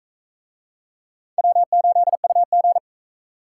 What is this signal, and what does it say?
Morse code, heard as a keyed tone, about 44 wpm, 700 Hz. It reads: W8UG